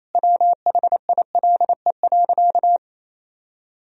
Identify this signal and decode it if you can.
Morse code, heard as a keyed tone, about 28 wpm, 700 Hz. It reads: WHILE.
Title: Sentences from Top 500 Words and Abbreviations 28wpm